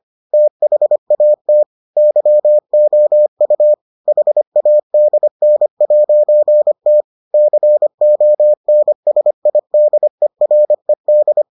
Morse code, heard as a keyed tone, about 25 wpm, 600 Hz. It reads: THAT YOU HADN'T CONSIDERED